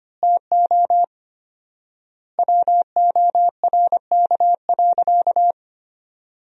Morse code, heard as a keyed tone, about 25 wpm, 700 Hz. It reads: TO WORK.
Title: 4x Word-Spacing - Sentences from Top 300 Words - 25wpm